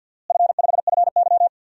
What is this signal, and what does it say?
Morse code, heard as a keyed tone, about 49 wpm, 700 Hz. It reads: F5FY
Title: DXCC-100 Call Signs - 4 Characters - Repeated 2x - 50wpm